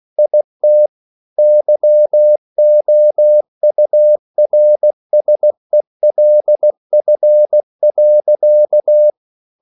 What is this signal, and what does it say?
Morse code, heard as a keyed tone, about 16 wpm, 600 Hz. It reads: IT YOURSELF.
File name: CW/Practice